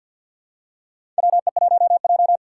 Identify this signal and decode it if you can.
Morse code, heard as a keyed tone, about 45 wpm, 700 Hz. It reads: WE1J
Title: USA Call Signs - 4 Characters - 2x Character Spacing - 50wpm